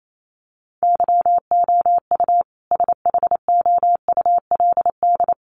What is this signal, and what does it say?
Morse code, heard as a keyed tone, about 28 wpm, 700 Hz. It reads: YOU SHOULD
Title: Sentences from Top 1200 Words 28wpm